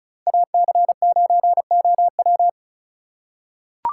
Morse code, heard as a keyed tone, about 36 wpm, 700 Hz. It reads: AC9OW
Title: USA Call Signs - 5 Characters - 2x Character Spacing - 35wpm